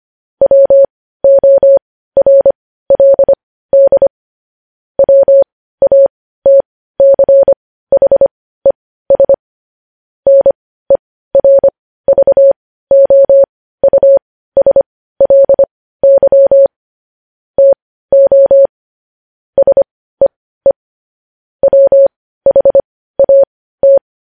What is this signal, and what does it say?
Morse code, heard as a keyed tone, about 25 wpm, spaced as for 15 wpm, 550 Hz. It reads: WORLD WATCHES NERVOUSLY TO SEE WHAT